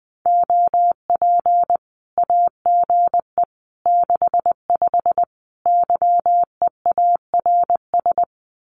Morse code, heard as a keyed tone, about 20 wpm, 700 Hz. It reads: OP AGE 65 YEARS
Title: Common QSO Phrases 20wpm